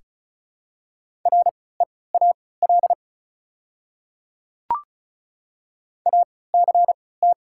Morse code, heard as a keyed tone, about 36 wpm, spaced as for 20 wpm, 700 Hz. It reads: REAL ACT
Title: 3x Character Spacing - Top 300 Words - 35wpm